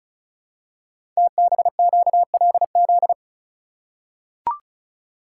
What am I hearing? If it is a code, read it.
Morse code, heard as a keyed tone, about 35 wpm, 700 Hz. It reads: TBQLZ
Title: Code Groups - 5 Letter:Numbers - 35wpm